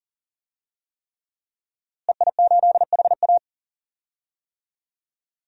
Morse code, heard as a keyed tone, about 39 wpm, 700 Hz. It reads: EI8HA
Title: DXCC-100 Call Signs - 5 Characters - Repeated 3x - 40wpm